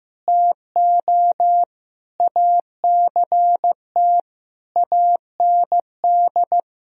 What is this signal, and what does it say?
Morse code, heard as a keyed tone, about 15 wpm, 700 Hz. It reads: TO ACT AND
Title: Sentences from Top 1000 Words - Review Part 1 - 15wpm